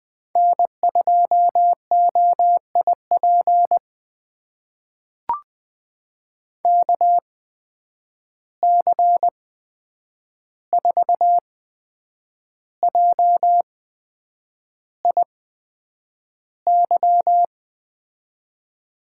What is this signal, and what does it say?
Morse code, heard as a keyed tone, about 20 wpm, spaced as for 8 wpm, 700 Hz. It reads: N2OIP KC4JIY
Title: USA Call Signs - 8x Character Spacing - 20wpm